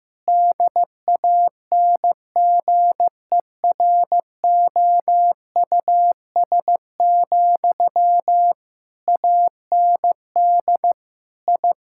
Morse code, heard as a keyed tone, about 15 wpm, 700 Hz. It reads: DANGEROUS, AND I
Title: Sentences from Top 1600 Words 15wpm